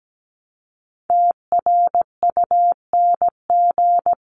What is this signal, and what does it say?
Morse code, heard as a keyed tone, about 17 wpm, 700 Hz. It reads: TRUNG
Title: CWT Exchange 17wpm